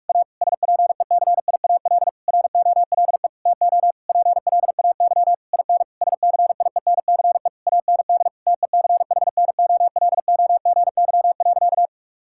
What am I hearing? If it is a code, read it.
Morse code, heard as a keyed tone, about 46 wpm, 700 Hz. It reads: A SPECIAL ROLE TO PLAY IN SCIENCE AND TECHNOLOGY.
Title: Sentences from Top 1000 Words - Review Part 6 - 45wpm